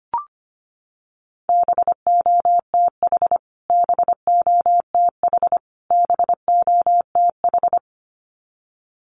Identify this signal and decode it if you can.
Morse code, heard as a keyed tone, about 25 wpm, 700 Hz. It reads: BOTH BOTH BOTH
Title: Top 500 Words Repeated 25wpm